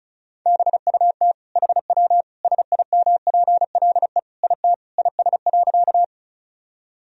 Morse code, heard as a keyed tone, about 34 wpm, 700 Hz. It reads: BUT HW SIMPLE IT IS.
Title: Sentences from Top 400 Words and Abbreviations 35wpm